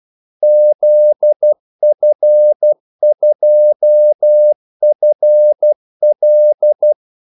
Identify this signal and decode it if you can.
Morse code, heard as a keyed tone, about 12 wpm, 600 Hz. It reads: ZF2FL
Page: 2 ekstra ordmellemrum mellem kaldesignalerne